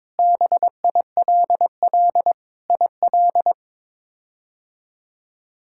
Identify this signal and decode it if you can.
Morse code, heard as a keyed tone, about 22 wpm, 700 Hz. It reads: BILL IL
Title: K1USN SST QSO Practice - 22wpm